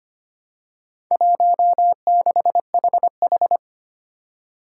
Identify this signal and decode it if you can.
Morse code, heard as a keyed tone, about 25 wpm, 700 Hz. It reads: 16HH
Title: Four Challenging Letter-Number Combinations 25wpm